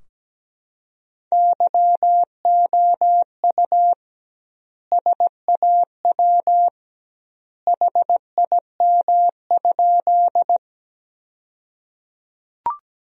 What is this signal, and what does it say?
Morse code, heard as a keyed tone, about 17 wpm, 700 Hz. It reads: YOU SAW HIM?
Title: 2x Word-Spacing - Sentences from Top 300 Words - 17wpm